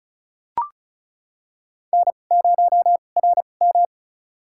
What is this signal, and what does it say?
Morse code, heard as a keyed tone, about 34 wpm, spaced as for 25 wpm, 700 Hz. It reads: N0RM
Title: USA Call Signs - 4 Characters - 2x Character Spacing - 35wpm